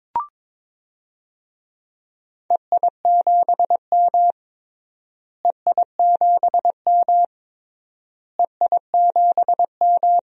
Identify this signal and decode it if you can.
Morse code, heard as a keyed tone, about 22 wpm, 700 Hz. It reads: EI7M EI7M EI7M
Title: DXCC-100 Call Signs - 4 Characters - Repeated 3x - 22wpm